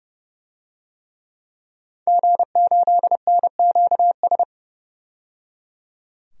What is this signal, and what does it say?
Morse code, heard as a keyed tone, about 30 wpm, 690 Hz. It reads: G8NQS
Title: DXCC-100 Call Signs - 5 Characters - 30wpm